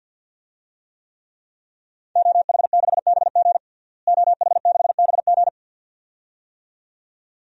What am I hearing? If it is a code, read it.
Morse code, heard as a keyed tone, about 53 wpm, 700 Hz. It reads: OH6BG OH6BG
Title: DXCC-100 Call Signs - 5 Characters - Repeated 2x - 50wpm